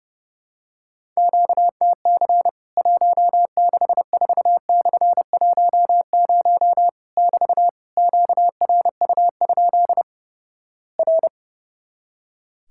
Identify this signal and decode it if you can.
Morse code, heard as a keyed tone, about 30 wpm, 700 Hz. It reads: QTC 164/10 <BT> QRU? R
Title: Worked All Europe DX Contest - QTC Exchange - 30wpm